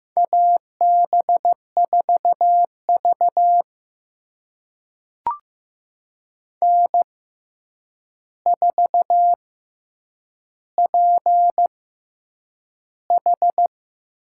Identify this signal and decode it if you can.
Morse code, heard as a keyed tone, about 15 wpm, spaced as for 8 wpm, 700 Hz. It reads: AB4V N4PH
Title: USA Call Signs - 4 Characters - 6x Character Spacing - 15wpm